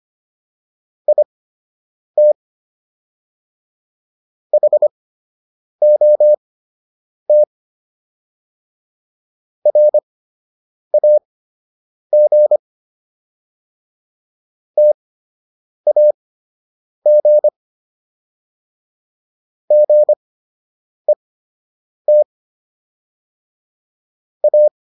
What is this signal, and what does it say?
Morse code, heard as a keyed tone, about 25 wpm, spaced as for 8 wpm, 600 Hz. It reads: IT HOT RAG TAG GET A